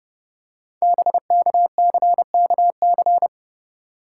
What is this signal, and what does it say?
Morse code, heard as a keyed tone, about 30 wpm, 700 Hz. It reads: BKCKC
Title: Code Groups - 5 Letters - 30wpm